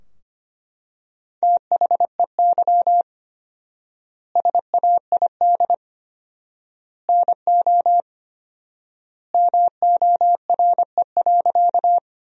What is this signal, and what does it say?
Morse code, heard as a keyed tone, about 25 wpm, 700 Hz. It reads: THEY SAID NO MORE.
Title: 4x Word-Spacing - Sentences from Top 100 Words - 25wpm